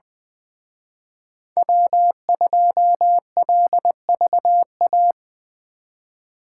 Morse code, heard as a keyed tone, about 24 wpm, 700 Hz. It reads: W2LVA
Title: Speed Racing - USA 1x3 Call Signs 20wpm